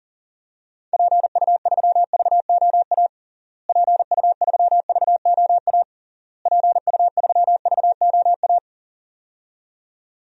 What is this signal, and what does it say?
Morse code, heard as a keyed tone, about 40 wpm, 700 Hz. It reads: PU3VOA PU3VOA PU3VOA
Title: DXCC-100 Call Signs - 6 Characters - Repeated 3x - 40wpm